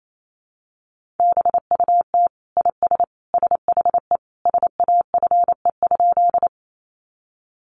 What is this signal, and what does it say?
Morse code, heard as a keyed tone, about 28 wpm, 700 Hz. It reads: BUT IS SHE SAFE?
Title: Sentences from Top 900 Words 28wpm